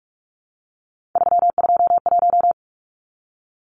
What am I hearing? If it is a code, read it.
Morse code, heard as a keyed tone, about 44 wpm, 700 Hz. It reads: 321
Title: Numbers - 3-Digit 45wpm